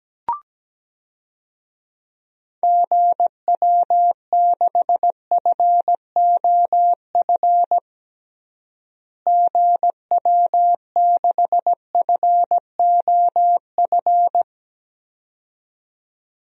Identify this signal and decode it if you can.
Morse code, heard as a keyed tone, about 17 wpm, 700 Hz. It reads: GW6FOF GW6FOF
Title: DXCC-100 Call Signs - 6 Characters - Repeated 2x - 17wpm